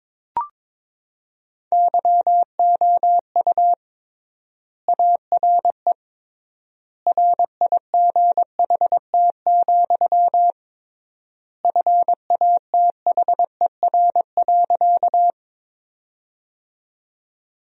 Morse code, heard as a keyed tone, about 22 wpm, 700 Hz. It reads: YOU ARE RIGHT, FATHER.
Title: 3x Word-Spacing - Sentences from Top 300 Words - 22wpm